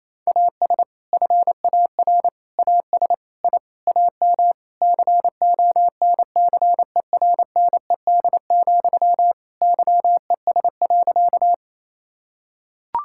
Morse code, heard as a keyed tone, about 28 wpm, 700 Hz. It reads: AS FAR AS I AM CONCERNED, YES.
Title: Sentences from Top 1000 Words - Review Part 6 - 28wpm